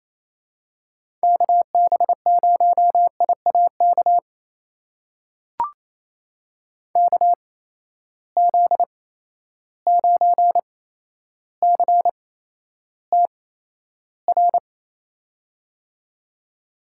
Morse code, heard as a keyed tone, about 28 wpm, spaced as for 12 wpm, 700 Hz. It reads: KB0IAK KZ9CTR
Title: USA Call Signs - 6 Characters - 8x Character Spacing - 28wpm